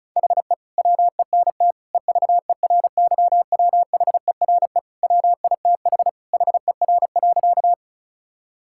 Morse code, heard as a keyed tone, about 34 wpm, 700 Hz. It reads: HE WENT EVERYWHERE WITH HER.
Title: Sentences from Top 1600 Words 35wpm